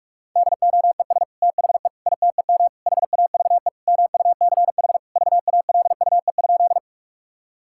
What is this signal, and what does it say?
Morse code, heard as a keyed tone, about 42 wpm, 700 Hz. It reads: DOES THE ITEM HAVE MUCH VALUE?
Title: Sentences from Top 1000 Words - Review Part 6 - 45wpm